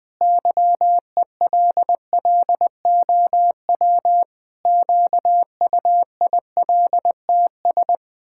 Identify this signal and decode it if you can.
Morse code, heard as a keyed tone, about 20 wpm, 700 Hz. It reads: YELLOW QUILTS
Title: Pangram Challenge v2 - 20wpm